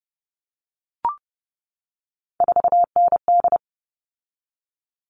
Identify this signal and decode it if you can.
Morse code, heard as a keyed tone, about 30 wpm, 700 Hz. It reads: 4ND
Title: All Cut Numbers - 3-Digit 30wpm